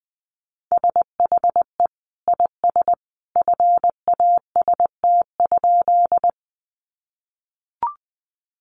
Morse code, heard as a keyed tone, about 20 wpm, 700 Hz. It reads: SHE IS FAST?
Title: Sentences from Top 400 Words 20wpm